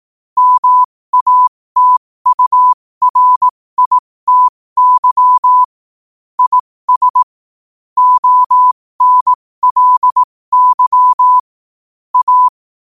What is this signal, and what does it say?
Morse code, heard as a keyed tone, about 18 wpm, spaced as for 15 wpm, 1000 Hz. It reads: MATURITY IS ONLY A
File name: qotdfiles / 15wpm